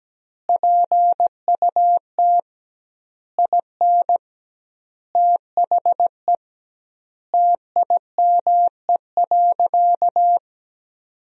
Morse code, heard as a keyed tone, about 17 wpm, 700 Hz. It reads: PUT IN THE TIME.
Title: 2x Word-Spacing - Sentences from Top 200 Words - 17wpm